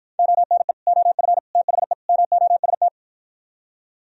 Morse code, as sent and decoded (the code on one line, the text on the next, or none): --- -. / --- ..-. / - .... / -- --- ... -
ONE OF THE MOST